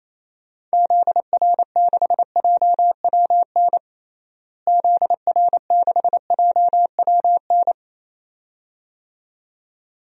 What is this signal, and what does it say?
Morse code, heard as a keyed tone, about 28 wpm, 700 Hz. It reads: ZR6JWN ZR6JWN
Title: DXCC-100 Call Signs - 6 Characters - Repeated 2x - 28wpm